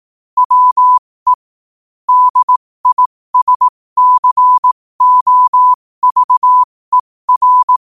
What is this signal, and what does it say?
Morse code, heard as a keyed tone, about 18 wpm, spaced as for 15 wpm, 1000 Hz. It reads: WE DISCOVER
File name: qotdfiles / 15wpm